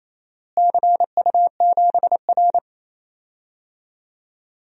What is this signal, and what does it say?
Morse code, heard as a keyed tone, about 27 wpm, 700 Hz. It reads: CU7R
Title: DXCC-100 Call Signs - 4 Characters - Repeated 2x - 28wpm